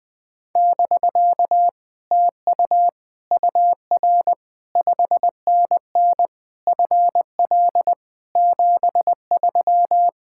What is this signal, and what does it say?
Morse code, heard as a keyed tone, about 20 wpm, 700 Hz. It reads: <BK> TU UR 5NN FL 73
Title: Speed-Racing Common QSO Phrases - 20wpm